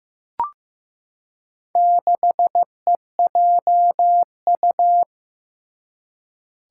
Morse code, heard as a keyed tone, about 15 wpm, 700 Hz. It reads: 6EJU
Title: Four Challenging Letter-Number Combinations 15wpm